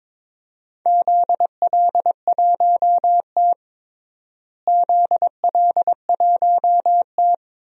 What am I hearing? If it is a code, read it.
Morse code, heard as a keyed tone, about 21 wpm, 700 Hz. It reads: ZL1T ZL1T
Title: DXCC-100 Call Signs - 4 Characters - Repeated 2x - 22wpm